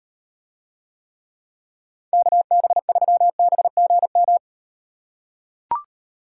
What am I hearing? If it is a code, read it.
Morse code, heard as a keyed tone, about 38 wpm, 700 Hz. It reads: KB3BGM